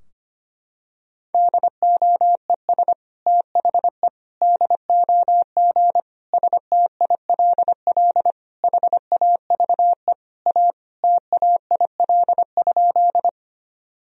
Morse code, heard as a keyed tone, about 25 wpm, 700 Hz. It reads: DOES THE DOG STILL HAVE A TAIL?
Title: Sentences from Top 1000 Words - Review Part 3 - 25wpm